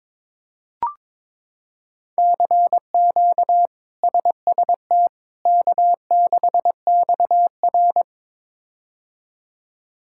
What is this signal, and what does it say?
Morse code, heard as a keyed tone, about 22 wpm, 700 Hz. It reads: CQ SST K6XR
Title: K1USN SST QSO Practice - 22wpm